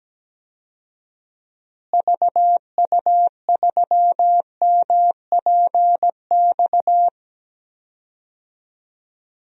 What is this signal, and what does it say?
Morse code, heard as a keyed tone, about 17 wpm, 700 Hz. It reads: VU3MPX
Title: DXCC-100 Call Signs - 6 Characters - Repeated 2x - 17wpm